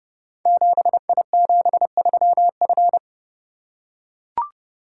Morse code, heard as a keyed tone, about 29 wpm, 700 Hz. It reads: 7I73F
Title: Code Groups - 5 Letter:Numbers - 30wpm